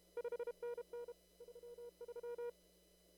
Morse code, heard as a keyed tone, about 33 wpm, 480 Hz. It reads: 5NN 33
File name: CW / kt5j wpx cw-op-k5pi